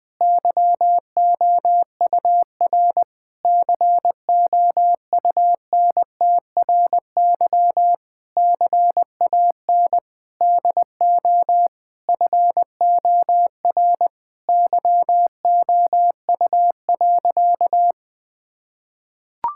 Morse code, heard as a keyed tone, about 20 wpm, 700 Hz. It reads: YOUR COUNTRY CAN DO FOR YOU.
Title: Sentences with 8-Word Phrases - 20wpm